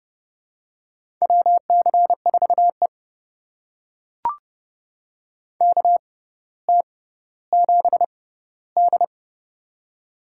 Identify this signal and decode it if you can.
Morse code, heard as a keyed tone, about 30 wpm, spaced as for 15 wpm, 700 Hz. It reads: WC4E KT7D
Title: USA Call Signs - 4 Characters - 6x Character Spacing - 30wpm